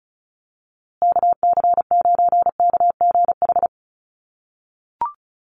Morse code, heard as a keyed tone, about 34 wpm, 700 Hz. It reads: KC9KGH
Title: USA Call Signs Repeated 3x 35wpm